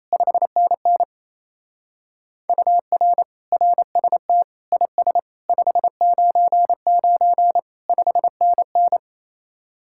Morse code, heard as a keyed tone, about 32 wpm, 700 Hz. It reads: 5NN UR RST IS 599 5NN
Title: Speed-Racing Common QSO Phrases - 28wpm